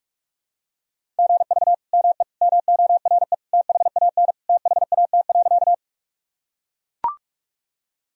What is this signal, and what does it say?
Morse code, heard as a keyed tone, about 44 wpm, 700 Hz. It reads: GV ME MORE THAN THAT.
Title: Sentences from Top 200 Words and Abbreviations 45wpm